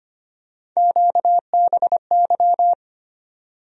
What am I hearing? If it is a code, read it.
Morse code, heard as a keyed tone, about 25 wpm, 700 Hz. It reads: QBY
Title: USA Call Sign Postfixes - 3 Letters 25wpm